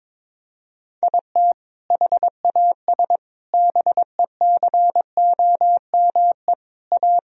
Morse code, heard as a keyed tone, about 22 wpm, 700 Hz. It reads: IT HAS BECOME A
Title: Sentences from Top 1000 Words - Review Part 3 - 22wpm